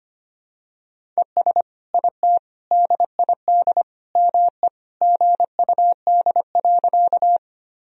Morse code, heard as a keyed tone, about 25 wpm, 700 Hz. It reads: ES IT DID ME GUD.
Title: Sentences from Top 200 Words and Abbreviations 25wpm